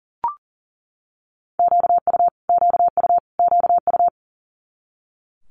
Morse code, heard as a keyed tone, about 38 wpm, 700 Hz. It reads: QU QU QU
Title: Multi-letter Phonemes Repeated 3x 40wpm